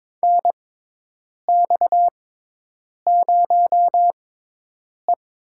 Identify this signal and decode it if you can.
Morse code, heard as a keyed tone, about 23 wpm, spaced as for 8 wpm, 700 Hz. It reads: NX0E
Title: USA Call Signs - 6x Character Spacing - 22wpm